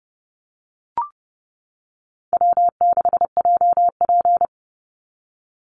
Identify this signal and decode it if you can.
Morse code, heard as a keyed tone, about 30 wpm, 700 Hz. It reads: W6JP
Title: NA SOTA Chaser Callsigns 30wpm